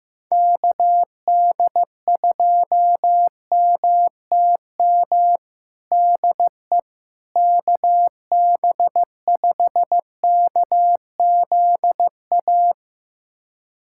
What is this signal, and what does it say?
Morse code, heard as a keyed tone, about 16 wpm, 700 Hz. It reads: KD2MTM DE KB5KZA
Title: USA Callsign DE Callsign - 6 Characters - 15wpm